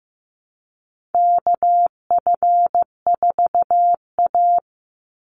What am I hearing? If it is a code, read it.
Morse code, heard as a keyed tone, about 15 wpm, 700 Hz. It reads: KF4A
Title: Call Signs Repeated 2x - 15wpm